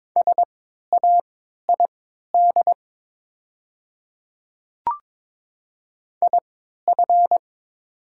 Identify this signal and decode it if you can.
Morse code, heard as a keyed tone, about 22 wpm, spaced as for 12 wpm, 700 Hz. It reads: SAID IF
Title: 3x Character Spacing - Top 100 Words - 22wpm